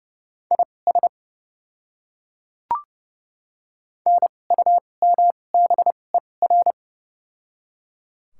Morse code, heard as a keyed tone, about 29 wpm, spaced as for 22 wpm, 700 Hz. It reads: IS NUMBER
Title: 2x Character Spacing - Top 100 Words - 30wpm